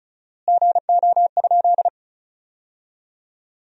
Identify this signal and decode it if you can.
Morse code, heard as a keyed tone, about 36 wpm, 700 Hz. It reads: GO?
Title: 4x Word-Spacing - Sentences from Top 200 Words - 35wpm